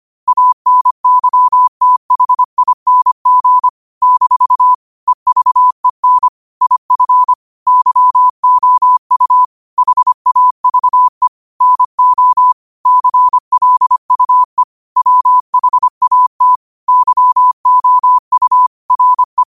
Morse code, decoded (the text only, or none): ANYTHING - EVEN IF YOU HAVE NO CLUE WHAT YOU RE